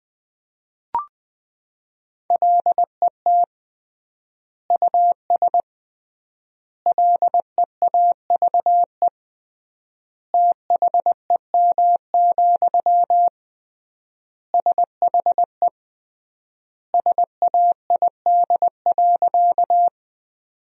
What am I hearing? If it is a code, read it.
Morse code, heard as a keyed tone, about 20 wpm, 700 Hz. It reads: LET US LEAVE THEM, SHE SAID.
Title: 3x Word-Spacing - Sentences from Top 400 Words - 20wpm